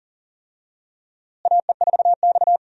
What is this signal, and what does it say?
Morse code, heard as a keyed tone, about 40 wpm, 700 Hz. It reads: AE4X
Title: USA Call Signs - 4 Characters - Repeated 3x - 40wpm